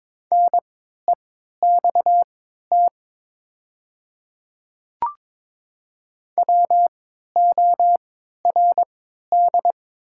Morse code, heard as a keyed tone, about 22 wpm, spaced as for 13 wpm, 700 Hz. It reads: NEXT WORD
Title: 3x Character Spacing - Top 300 Words - 22wpm